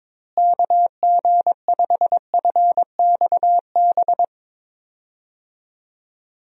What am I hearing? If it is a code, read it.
Morse code, heard as a keyed tone, about 22 wpm, 700 Hz. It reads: KG5FXB
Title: USA Call Signs - 6 Characters - Repeated 3x - 22wpm